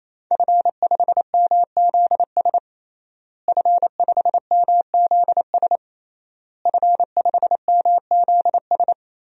Morse code, heard as a keyed tone, about 28 wpm, 700 Hz. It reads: F5MZS F5MZS F5MZS
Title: DXCC-100 Call Signs - 5 Characters - Repeated 3x - 28wpm